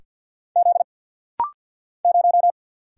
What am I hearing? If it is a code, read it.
Morse code, heard as a keyed tone, about 50 wpm, 700 Hz. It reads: Z 0
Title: Single Letter-Number - Rapid-Fire 50wpm